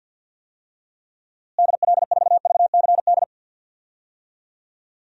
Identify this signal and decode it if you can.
Morse code, heard as a keyed tone, about 50 wpm, 700 Hz. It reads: DL4VCD
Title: DXCC-100 Call Signs - 6 Characters - Repeated 2x - 50wpm